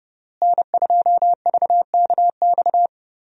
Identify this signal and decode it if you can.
Morse code, heard as a keyed tone, about 30 wpm, 700 Hz. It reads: N2VKX
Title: USA Call Signs - 5 Characters - Repeated 2x - 30wpm